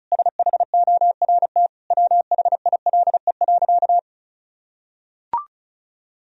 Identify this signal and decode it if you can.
Morse code, heard as a keyed tone, about 36 wpm, 700 Hz. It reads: SHORT WHILE.
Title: Sentences from Top 1000 Words - Review Part 2 - 35wpm